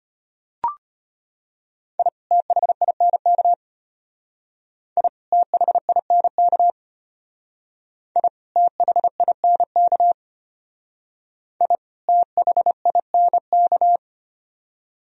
Morse code, heard as a keyed tone, about 32 wpm, 700 Hz. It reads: I THINK I THINK I THINK I THINK